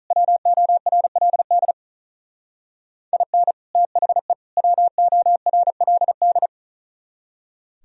Morse code, heard as a keyed tone, about 42 wpm, 700 Hz. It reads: WORLD IN THE WORLD
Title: Speed-Racing - Sets of 3 Words 35wpm